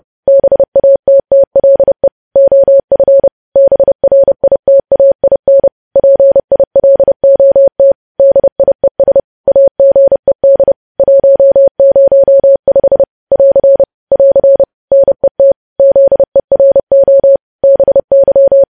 Morse code, heard as a keyed tone, about 30 wpm, 550 Hz. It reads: BATTLE OF BRITAIN PILOT DIES AGED 105 <AR> <AR> NET ZERO BY
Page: news headlines in morse code at 30 wpm